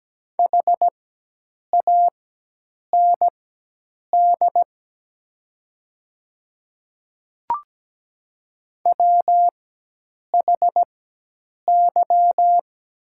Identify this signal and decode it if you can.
Morse code, heard as a keyed tone, about 17 wpm, spaced as for 8 wpm, 690 Hz. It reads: HAND WHY
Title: 4x Character Spacing - Top 200 Words - 17wpm